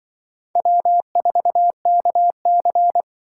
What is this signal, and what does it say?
Morse code, heard as a keyed tone, about 23 wpm, 700 Hz. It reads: W4KC
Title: Speed Racing - USA 1x2 Call Signs 20wpm